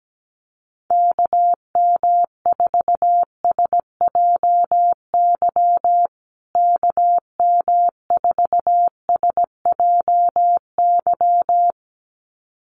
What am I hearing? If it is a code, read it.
Morse code, heard as a keyed tone, about 17 wpm, 700 Hz. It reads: KM4SJY KM4SJY
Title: USA Call Signs Repeated 2x 17wpm